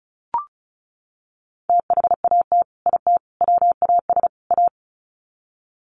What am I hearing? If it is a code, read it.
Morse code, heard as a keyed tone, about 34 wpm, 700 Hz. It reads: THAT IT WAS A
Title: Sets of 4 Words 35wpm